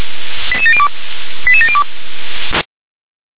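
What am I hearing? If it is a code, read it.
Five-tone selective call (ZVEI), heard as quick runs of five tones, tones of about 70 ms. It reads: 88702 88702
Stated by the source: ZVEI five-tone selective call received on an SDR.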